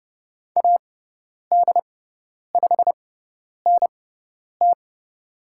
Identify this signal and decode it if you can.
Morse code, heard as a keyed tone, about 30 wpm, spaced as for 10 wpm, 700 Hz. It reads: AD5NT
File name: Lesson 16g - Letter c - Call Signs 30wpm